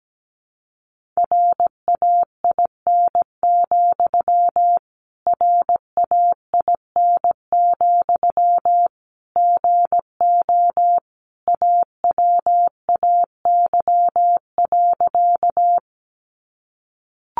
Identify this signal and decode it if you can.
Morse code, heard as a keyed tone, about 17 wpm, 700 Hz. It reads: RAIN, RAIN, GO AWAY.
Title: Sentences from Top 600 Words 17wpm